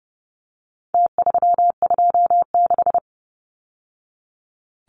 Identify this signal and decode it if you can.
Morse code, heard as a keyed tone, about 30 wpm, 700 Hz. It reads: T326
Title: Numbers - Contest Style - 4-Digit 30wpm